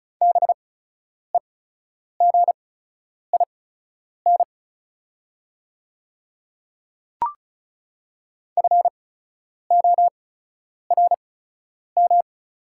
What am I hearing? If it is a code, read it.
Morse code, heard as a keyed tone, about 36 wpm, spaced as for 10 wpm, 700 Hz. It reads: BEGIN FORM
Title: 8x Character Spacing - Top 300 Words - 35wpm